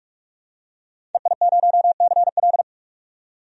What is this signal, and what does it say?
Morse code, heard as a keyed tone, about 47 wpm, 700 Hz. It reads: EI0CL
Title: DXCC-100 Call Signs - 5 Characters - 45wpm